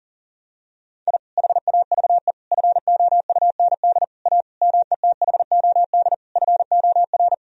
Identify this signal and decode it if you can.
Morse code, heard as a keyed tone, about 40 wpm, 700 Hz. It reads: I HAVE FOUND A METHOD FOR
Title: Sentences from Top 1000 Words - Review Part 3 - 40wpm